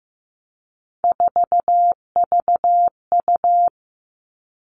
Morse code, heard as a keyed tone, about 15 wpm, 700 Hz. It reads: 4VU
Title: All Cut Numbers - 3-Digit 15wpm